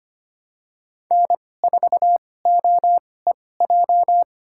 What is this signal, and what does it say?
Morse code, heard as a keyed tone, about 25 wpm, spaced as for 18 wpm, 700 Hz. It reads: N4OEJ
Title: USA Call Signs - 5 Characters - 2x Character Spacing - 25wpm